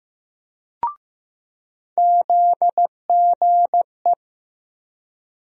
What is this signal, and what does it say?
Morse code, heard as a keyed tone, about 15 wpm, 700 Hz. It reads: ZGE
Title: USA Call Sign Postfixes - 3 Letters 15wpm